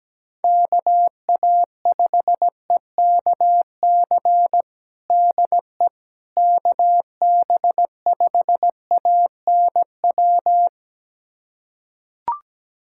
Morse code, heard as a keyed tone, about 17 wpm, 700 Hz. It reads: KA5EKC DE KB5ANW
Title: USA Callsign DE Callsign - 6 Characters - 17wpm